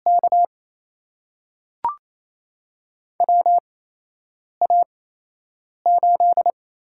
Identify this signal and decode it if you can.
Morse code, heard as a keyed tone, about 29 wpm, spaced as for 8 wpm, 700 Hz. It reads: K WA8
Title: USA Call Signs - 4 Characters - 8x Character Spacing - 28wpm